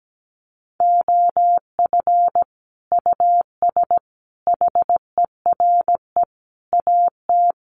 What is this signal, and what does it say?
Morse code, heard as a keyed tone, about 17 wpm, 700 Hz. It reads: OF US HERE AT
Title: Sets of 5 Words 17wpm